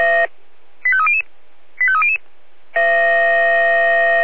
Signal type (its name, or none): five-tone selective call